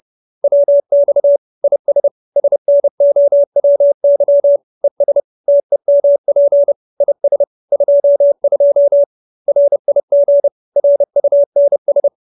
Morse code, heard as a keyed tone, about 30 wpm, 570 Hz. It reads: WX IS SNOWY ES TEMP IS 22 RIG RUNS